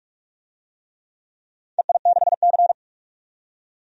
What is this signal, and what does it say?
Morse code, heard as a keyed tone, about 44 wpm, 690 Hz. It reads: EI6C
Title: DXCC-100 Call Signs - 4 Characters - 45wpm